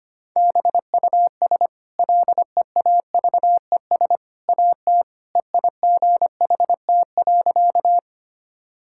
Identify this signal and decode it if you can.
Morse code, heard as a keyed tone, about 25 wpm, 690 Hz. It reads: BUS LEAVES AT EIGHT.
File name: Easy Sentences with Hints - 25wpm